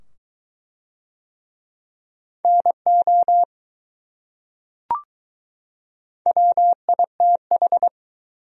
Morse code, heard as a keyed tone, about 20 wpm, 700 Hz. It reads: NO WITH